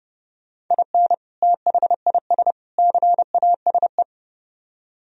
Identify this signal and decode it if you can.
Morse code, heard as a keyed tone, about 30 wpm, 700 Hz. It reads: IN THIS CASE